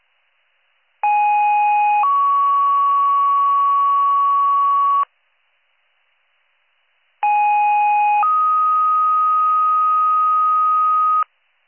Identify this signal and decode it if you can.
Two-tone page, heard as a one-second tone, then a three-second tone, about 830 then 1120 Hz, about 830 then 1250 Hz. It is heard from 1.0 to 11.2 s.